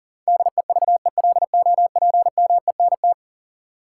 Morse code, heard as a keyed tone, about 40 wpm, 700 Hz. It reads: DEVELOPMENT
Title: Sets of 3 Words - New Word Drill - 40wpm